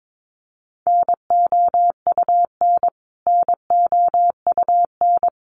Morse code, heard as a keyed tone, about 22 wpm, 700 Hz. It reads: NOUN NOUN
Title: Top 500 Words Repeated 22wpm